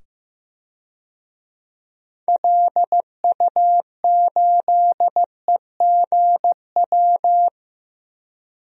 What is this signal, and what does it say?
Morse code, heard as a keyed tone, about 15 wpm, 700 Hz. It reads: LU8EGW
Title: DXCC-100 Call Signs - 6 Characters - 15wpm